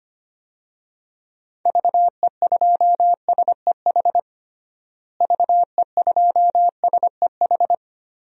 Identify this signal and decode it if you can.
Morse code, heard as a keyed tone, about 25 wpm, 700 Hz. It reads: VE2SEH VE2SEH
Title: DXCC-100 Call Signs - 6 Characters - Repeated 2x - 25wpm